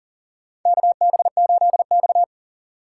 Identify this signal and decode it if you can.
Morse code, heard as a keyed tone, about 40 wpm, 700 Hz. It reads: KB8X
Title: USA Call Signs - 4 Characters - 6x Character Spacing - 40wpm